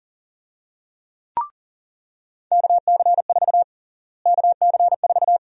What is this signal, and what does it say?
Morse code, heard as a keyed tone, about 40 wpm, 700 Hz. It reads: KC4 KC4
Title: USA Call Sign Prefixes - Repeated 2x - 40wpm